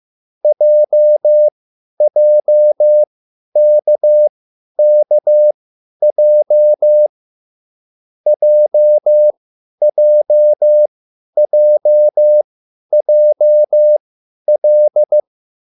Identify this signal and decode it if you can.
Morse code, heard as a keyed tone, about 15 wpm, spaced as for 10 wpm, 600 Hz. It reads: JJKKJ JJJJL